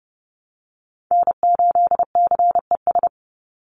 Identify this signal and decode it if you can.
Morse code, heard as a keyed tone, about 30 wpm, 700 Hz. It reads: N8CES
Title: USA Call Signs 30wpm